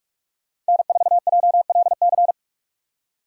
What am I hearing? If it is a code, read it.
Morse code, heard as a keyed tone, about 41 wpm, 700 Hz. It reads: N4JLC
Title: USA Call Signs - 5 Characters - 45wpm